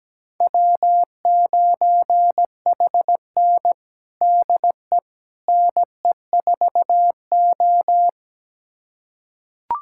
Morse code, heard as a keyed tone, about 17 wpm, 700 Hz. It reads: W9HN DE NE4O
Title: USA Callsign DE Callsign - 4 Characters - 17wpm